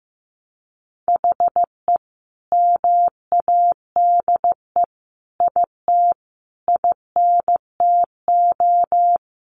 Morse code, heard as a keyed tone, about 15 wpm, 700 Hz. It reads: HE MADE IT INTO
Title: Sentences from Top 1300 Words 15wpm